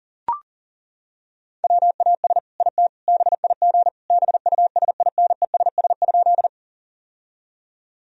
Morse code, heard as a keyed tone, about 40 wpm, 700 Hz. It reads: WAS IT BIG BUSINESS?
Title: Sentences from Top 1000 Words - Review Part 1 - 40wpm